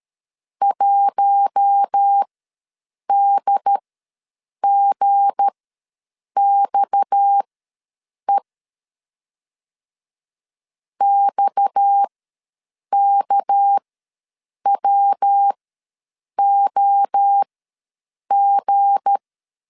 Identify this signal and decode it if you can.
Morse code, heard as a keyed tone, about 13 wpm, spaced as for 7 wpm, 800 Hz. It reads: 1DGXE XKWOG
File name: LEZIONI / 7 GF9Z / 7 G ws 12wpm 3140